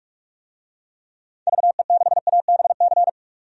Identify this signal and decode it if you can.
Morse code, heard as a keyed tone, about 45 wpm, 700 Hz. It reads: VE6ABC
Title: DXCC-100 Call Signs - 6 Characters - 45wpm